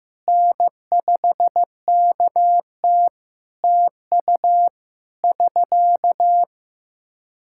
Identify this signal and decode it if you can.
Morse code, heard as a keyed tone, about 15 wpm, 700 Hz. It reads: N5KT TU <SK>
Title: Practice QSO - Exchange 4 - Ending - 15wpm